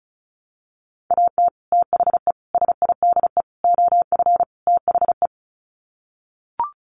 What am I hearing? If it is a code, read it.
Morse code, heard as a keyed tone, about 34 wpm, 700 Hz. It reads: AT THE SIDE OF THE
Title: Sets of 5 Words Encore2 35wpm